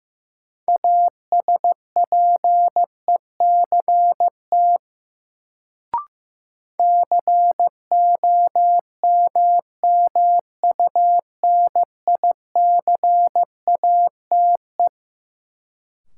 Morse code, heard as a keyed tone, about 15 wpm, 700 Hz. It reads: ASPECT COMMUNICATE
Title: Sentences from Top 1300 Words - New Word Drill - 15wpm